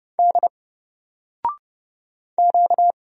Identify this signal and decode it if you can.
Morse code, heard as a keyed tone, about 31 wpm, 700 Hz. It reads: D Q
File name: Single-Letter-Number-30wpm